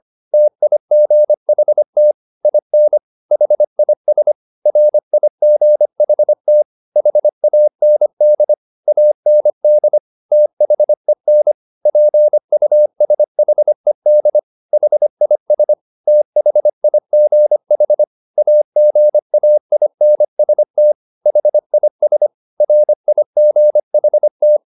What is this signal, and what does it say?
Morse code, heard as a keyed tone, about 25 wpm, 600 Hz. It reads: TIGHT IN HIS RIGHT HAND AND THEN PUSHED HIS THIGH AGAINST HIS RIGHT